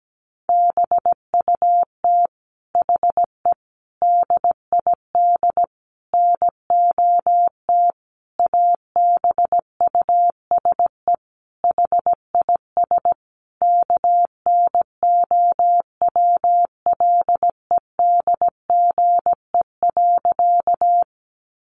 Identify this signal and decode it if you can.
Morse code, heard as a keyed tone, about 17 wpm, 700 Hz. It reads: BUT HE DID NOT ABUSE HIS KNOWLEDGE.
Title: Sentences from Top 1300 Words 17wpm